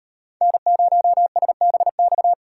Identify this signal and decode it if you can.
Morse code, heard as a keyed tone, about 38 wpm, 700 Hz. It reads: N0SBX